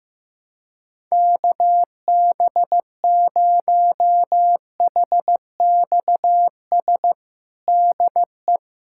Morse code, heard as a keyed tone, about 16 wpm, 700 Hz. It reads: KB0HXS DE
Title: USA Callsign DE Callsign - 6 Characters - 15wpm